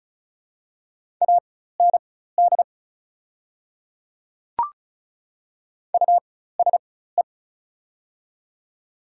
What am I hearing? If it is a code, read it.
Morse code, heard as a keyed tone, about 34 wpm, spaced as for 16 wpm, 690 Hz. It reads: AND USE
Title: 4x Character Spacing - Top 100 Words - 35wpm